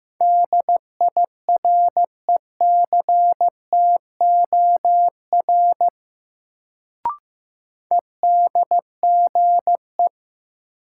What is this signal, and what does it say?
Morse code, heard as a keyed tone, about 15 wpm, 700 Hz. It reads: DIRECTOR EDGE
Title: Sets of 3 Words - New Word Drill - 15wpm